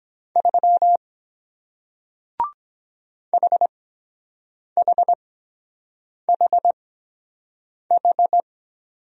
Morse code, heard as a keyed tone, about 23 wpm, spaced as for 7 wpm, 700 Hz. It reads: 3 HHHH